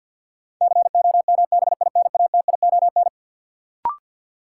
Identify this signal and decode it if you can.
Morse code, heard as a keyed tone, about 53 wpm, 700 Hz. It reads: COMBINATION
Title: Sentences from Top 1000 Words - Review Part 6 - New Word Drill - 50wpm